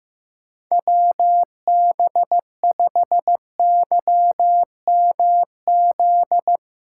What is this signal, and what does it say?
Morse code, heard as a keyed tone, about 15 wpm, 700 Hz. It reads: WB5YMZ
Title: USA Callsign DE Callsign 15wpm